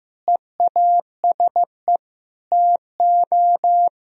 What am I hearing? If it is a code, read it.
Morse code, heard as a keyed tone, about 15 wpm, 700 Hz. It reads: EASE TO
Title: Sentences from Top 1000 Words - Review Part 1 - 15wpm